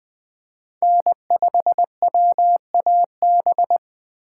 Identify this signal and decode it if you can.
Morse code, heard as a keyed tone, about 20 wpm, 700 Hz. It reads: N5WAB
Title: USA Call Signs - 5 Characters - 2x Character Spacing - 20wpm